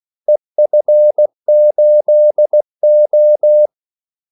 Morse code, heard as a keyed tone, about 16 wpm, 600 Hz. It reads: EF8O
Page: Kaldesignaler 16 wpm